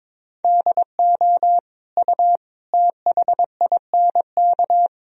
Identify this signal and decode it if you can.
Morse code, heard as a keyed tone, about 22 wpm, 700 Hz. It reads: DO U THINK
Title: Sentences from Top 200 Words and Abbreviations 22wpm